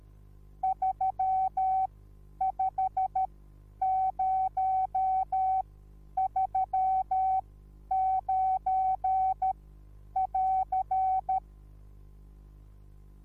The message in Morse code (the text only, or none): 35039<AR>